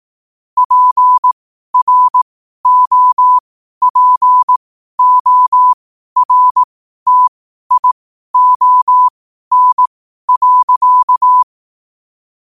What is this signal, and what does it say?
Morse code, heard as a keyed tone, about 18 wpm, spaced as for 12 wpm, 1000 Hz. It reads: PROPORTION.
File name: qotdfiles / 12wpm